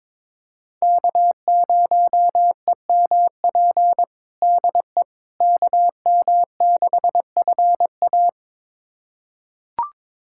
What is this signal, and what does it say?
Morse code, heard as a keyed tone, about 22 wpm, 700 Hz. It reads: K0EMP DE KM6FA
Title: USA Callsign DE Callsign - 5 Characters - 22wpm